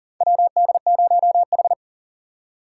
Morse code, heard as a keyed tone, about 40 wpm, 700 Hz. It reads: WD0H
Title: USA Call Signs - 4 Characters - 3x Character Spacing - 40wpm